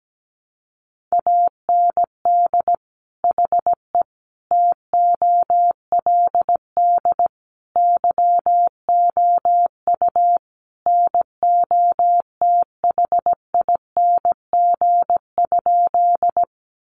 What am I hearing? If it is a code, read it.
Morse code, heard as a keyed tone, about 17 wpm, 700 Hz. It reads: AND HE TOLD YOU NOTHING?
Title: Sentences from Top 500 Words 17wpm